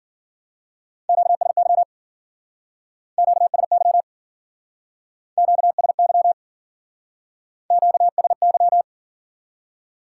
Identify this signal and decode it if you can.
Morse code, heard as a keyed tone, about 53 wpm, 700 Hz. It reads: QSY QSY QSY QSY